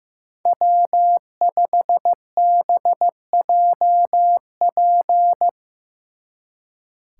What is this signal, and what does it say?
Morse code, heard as a keyed tone, about 17 wpm, 700 Hz. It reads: W5BJP
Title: Speed Racing - USA 1x3 Call Signs 15wpm